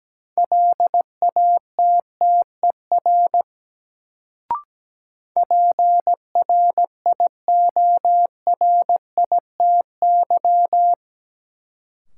Morse code, heard as a keyed tone, about 17 wpm, 700 Hz. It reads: LATTER PRIORITY
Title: Sentences from Top 1600 Words - New Word Drill Alt - 17wpm